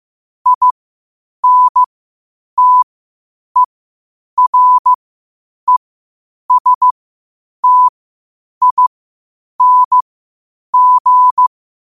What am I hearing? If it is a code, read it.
Morse code, heard as a keyed tone, about 15 wpm, spaced as for 8 wpm, 1000 Hz. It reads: INTERESTING
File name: qotdfiles / 8wpm